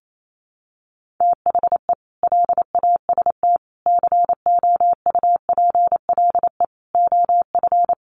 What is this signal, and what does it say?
Morse code, heard as a keyed tone, about 28 wpm, 700 Hz. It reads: THE LAST COUPLE OF
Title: Sets of 5 Words Encore2 28wpm